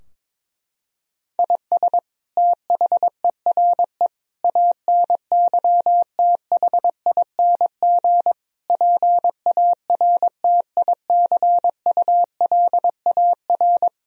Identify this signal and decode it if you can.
Morse code, heard as a keyed tone, about 22 wpm, 700 Hz. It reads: IS THERE ANYTHING PARTICULAR
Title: Sentences from Top 1000 Words - Review Part 4 - 22wpm